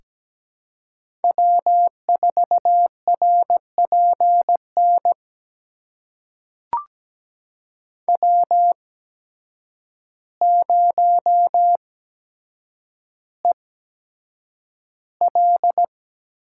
Morse code, heard as a keyed tone, about 17 wpm, spaced as for 8 wpm, 700 Hz. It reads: W4RPN W0EL
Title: USA Call Signs - 5 Characters - 8x Character Spacing - 17wpm